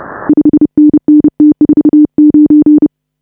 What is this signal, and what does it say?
Morse code, heard as a keyed tone, about 36 wpm, 310 Hz. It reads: 5NN T49
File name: AUDIO2019 / HA5BA CW